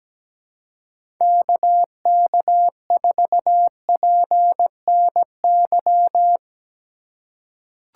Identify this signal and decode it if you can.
Morse code, heard as a keyed tone, about 17 wpm, 700 Hz. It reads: KK4PNY
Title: USA Call Signs - 6 Characters - 2x Character Spacing - 17wpm